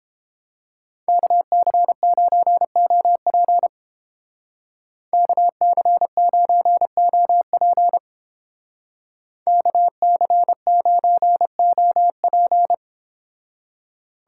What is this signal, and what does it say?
Morse code, heard as a keyed tone, about 31 wpm, 700 Hz. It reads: KC9OP KC9OP KC9OP